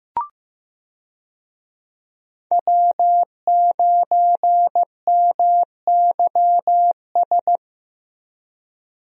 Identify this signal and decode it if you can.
Morse code, heard as a keyed tone, about 15 wpm, 700 Hz. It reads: W9MYS
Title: USA Call Signs - 5 Characters - Repeated 2x - 15wpm